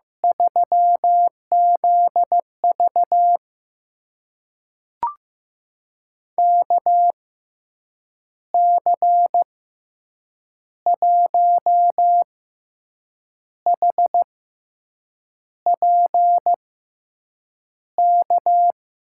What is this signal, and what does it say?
Morse code, heard as a keyed tone, about 15 wpm, spaced as for 6 wpm, 700 Hz. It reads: 3ZV KC1HPK
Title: USA Call Signs - 6x Character Spacing - 15wpm